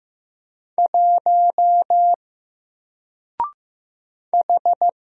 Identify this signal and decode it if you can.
Morse code, heard as a keyed tone, about 15 wpm, 700 Hz. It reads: 1 H
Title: Single Character (Letter:Number:Punctuation:Prosign) 15wpm